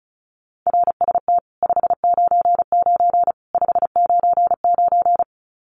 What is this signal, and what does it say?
Morse code, heard as a keyed tone, about 34 wpm, 700 Hz. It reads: RST 599 599
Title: Common QSO Phrases 35wpm